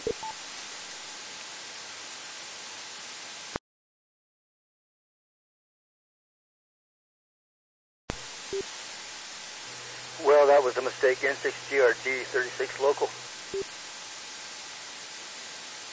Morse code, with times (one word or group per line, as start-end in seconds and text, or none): none